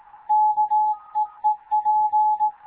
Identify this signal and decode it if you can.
Morse code, heard as a keyed tone, about 18 wpm, 850 Hz. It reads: KEEP
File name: Morze-HA2MN / hangp / 2 11 GYAKORLAT